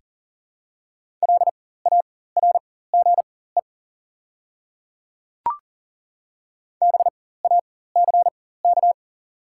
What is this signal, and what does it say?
Morse code, heard as a keyed tone, about 40 wpm, spaced as for 19 wpm, 700 Hz. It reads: LARGE BACK
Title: 4x Character Spacing - Top 300 Words - 40wpm